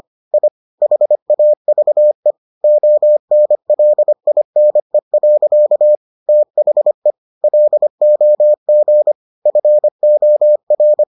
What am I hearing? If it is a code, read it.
Morse code, heard as a keyed tone, about 25 wpm, 600 Hz. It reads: I HAVE ONLINE. THE LOG FOR